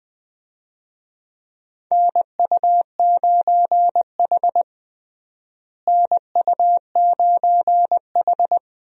Morse code, heard as a keyed tone, about 20 wpm, 700 Hz. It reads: NU9H NU9H
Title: USA Call Signs - 4 Characters - Repeated 3x - 20wpm